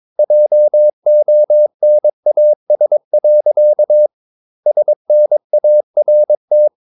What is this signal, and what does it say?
Morse code, heard as a keyed tone, about 22 wpm, 600 Hz. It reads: JONAS. SNART
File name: Ferie QRS 22wpm